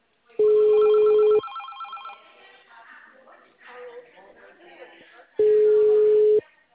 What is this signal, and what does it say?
Ringback tone (Europe, 425 Hz), with speech.